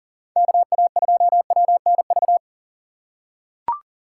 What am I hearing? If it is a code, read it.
Morse code, heard as a keyed tone, about 38 wpm, 700 Hz. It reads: KA2WNV
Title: USA Call Signs - 6 Characters - 4x Character Spacing - 40wpm